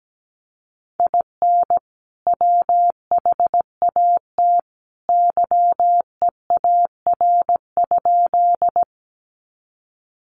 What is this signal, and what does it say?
Morse code, heard as a keyed tone, about 17 wpm, 700 Hz. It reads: IN WHAT YEAR?
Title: Sentences from Top 200 Words 17wpm